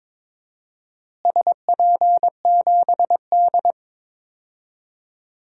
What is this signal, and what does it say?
Morse code, heard as a keyed tone, about 22 wpm, 700 Hz. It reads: SP7D
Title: DXCC-100 Call Signs - 4 Characters - Repeated 2x - 22wpm